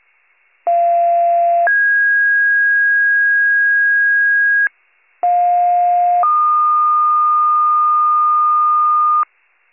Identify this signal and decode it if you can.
Two-tone page, heard as a one-second tone, then a three-second tone, about 690 then 1650 Hz, about 710 then 1220 Hz, with noise. It is heard from 0.7 to 9.2 s.